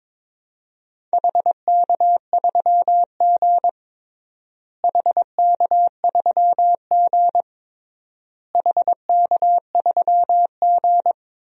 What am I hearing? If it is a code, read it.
Morse code, heard as a keyed tone, about 21 wpm, 700 Hz. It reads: HK3G HK3G HK3G
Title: DXCC-100 Call Signs - 4 Characters - Repeated 3x - 22wpm